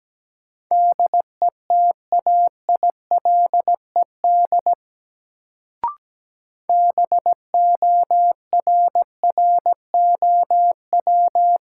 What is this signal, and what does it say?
Morse code, heard as a keyed tone, about 17 wpm, 700 Hz. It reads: DETAILED BORROW
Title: Sentences from Top 1500 Words - New Word Drill - 17wpm